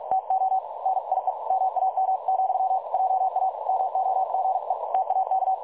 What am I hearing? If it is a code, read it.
Morse code, heard as a keyed tone, about 46 wpm, 760 Hz, with noise. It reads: TO NIE MAM, JA MOGE TAK
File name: dlazaawansowanych / nagraniazpasma / websdr recording 3565.0kHz